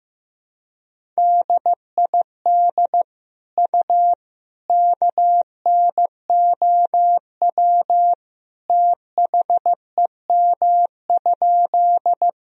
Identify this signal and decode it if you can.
Morse code, heard as a keyed tone, about 15 wpm, 700 Hz. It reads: DID U KNOW THEM?
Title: Sentences from Top 100 Words and Abbreviations 15wpm